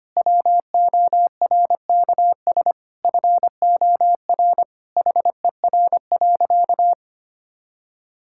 Morse code, heard as a keyed tone, about 25 wpm, 700 Hz. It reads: WORKS FOR HER.
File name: practice / 25wpm / Sentences with 4-Word Phrases - 25wpm